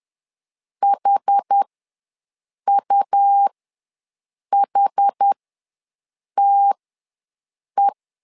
Morse code, heard as a keyed tone, about 10 wpm, spaced as for 6 wpm, 800 Hz. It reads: HUHTE
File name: LEZIONI / 2 UHDN / 2 UH ws 10wpm 2029